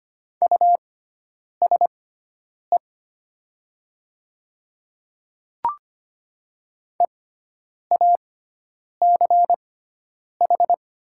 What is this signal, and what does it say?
Morse code, heard as a keyed tone, about 25 wpm, spaced as for 9 wpm, 700 Hz. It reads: USE EACH
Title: 6x Character Spacing - Top 200 Words - 25wpm